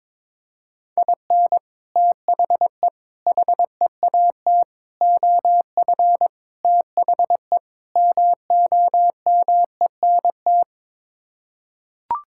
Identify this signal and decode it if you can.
Morse code, heard as a keyed tone, about 21 wpm, 700 Hz. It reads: IN THE HEAT OF THE MOMENT
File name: practice / 22wpm / Sets of 6 Words - 22wpm